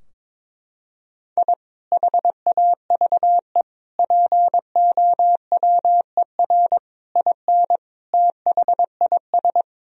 Morse code, heard as a keyed tone, about 22 wpm, 700 Hz. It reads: I HAVE POWER IN THIS
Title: Sentences from Top 1000 Words - Review Part 2 - 22wpm